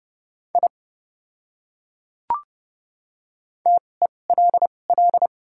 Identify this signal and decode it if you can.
Morse code, heard as a keyed tone, about 30 wpm, spaced as for 22 wpm, 700 Hz. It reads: I TELL
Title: 2x Character Spacing - Top 200 Words - 30wpm